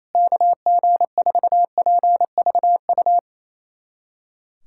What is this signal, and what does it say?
Morse code, heard as a keyed tone, about 27 wpm, 700 Hz. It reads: KG4PVU
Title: USA Callsign DE Callsign - 6 Characters - 28wpm